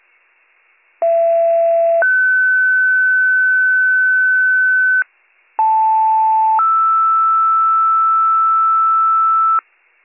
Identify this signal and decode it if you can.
Two-tone page, heard as a one-second tone, then a three-second tone, about 670 then 1500 Hz, about 880 then 1320 Hz, with noise. It is heard from 1.0 to 9.6 s.